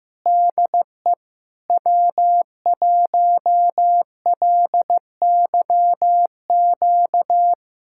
Morse code, heard as a keyed tone, about 16 wpm, 700 Hz. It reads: DE W1LYQ
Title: SOTA QSO Practice 15wpm